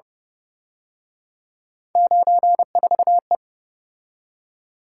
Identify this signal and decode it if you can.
Morse code, heard as a keyed tone, about 30 wpm, 700 Hz. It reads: 94E
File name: Speed-Racing - Three Letter-Number Combinations - 25wpm